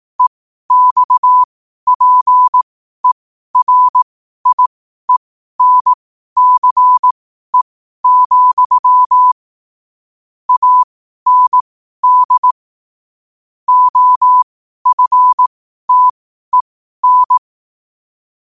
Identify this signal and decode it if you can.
Morse code, heard as a keyed tone, about 18 wpm, spaced as for 12 wpm, 1000 Hz. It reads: EXPERIENCE, AND OFTEN
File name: qotdfiles / 12wpm